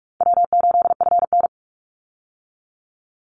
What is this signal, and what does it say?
Morse code, heard as a keyed tone, about 44 wpm, 700 Hz. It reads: W8FN
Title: Easy Call Signs Repeated 2x - 45wpm